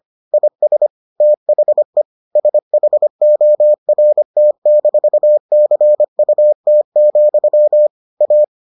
Morse code, heard as a keyed tone, about 25 wpm, 600 Hz. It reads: IS THE SHORT-CUT, A